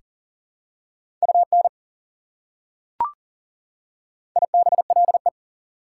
Morse code, heard as a keyed tone, about 40 wpm, 700 Hz. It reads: UN IBLE
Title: Prefixes and Suffixes 40wpm